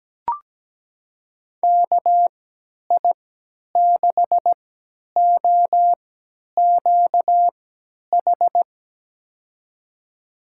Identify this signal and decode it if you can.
Morse code, heard as a keyed tone, about 17 wpm, spaced as for 10 wpm, 700 Hz. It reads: KI6OQH
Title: USA Call Signs - 6 Characters - 3x Character Spacing - 17wpm